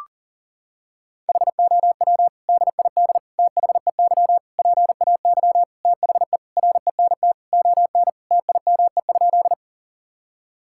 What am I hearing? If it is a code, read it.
Morse code, heard as a keyed tone, about 40 wpm, 700 Hz. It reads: HOW DID THEY PAY THE RENT ON TIME?
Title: Sentences from Top 1000 Words - Review Part 5 - 40wpm